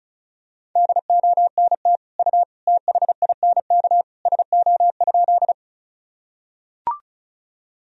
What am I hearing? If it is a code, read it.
Morse code, heard as a keyed tone, about 36 wpm, 700 Hz. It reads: DONT U THINK SO?
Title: Sentences from Top 300 Words and Abbreviations 35wpm